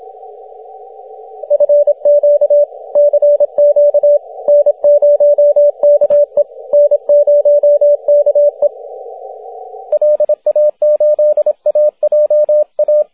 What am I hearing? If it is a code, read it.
Morse code, heard as a keyed tone, about 27 wpm, 580 Hz, with radio noise. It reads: FQ CQ N0KE N0KE LA8AJA
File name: wp / n0ke 6m cw june 7th 2010